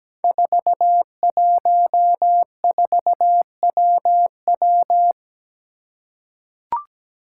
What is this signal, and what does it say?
Morse code, heard as a keyed tone, about 17 wpm, 700 Hz. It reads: 414WW
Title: Five Challenging Letter-Number Combinations 17wpm